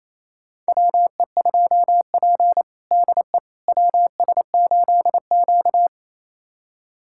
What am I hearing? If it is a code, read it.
Morse code, heard as a keyed tone, about 28 wpm, 700 Hz. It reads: WE2P DE WS8Q
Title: USA Callsign DE Callsign - 4 Characters - 28wpm